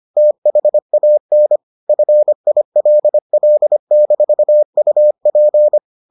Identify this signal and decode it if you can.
Morse code, heard as a keyed tone, about 25 wpm, 600 Hz. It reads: THAN FILL-UP